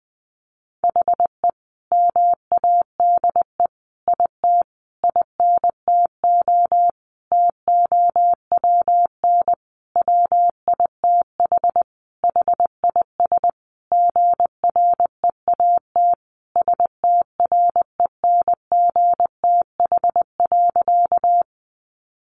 Morse code, heard as a keyed tone, about 19 wpm, 700 Hz. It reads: HE MADE IT INTO TOWN WITH HIS GREAT STRENGTH.
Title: Sentences from Top 1300 Words 20wpm